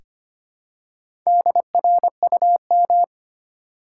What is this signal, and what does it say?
Morse code, heard as a keyed tone, about 25 wpm, 700 Hz. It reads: DRUM
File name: practice / 25wpm / Easy Words with Hints - 25wpm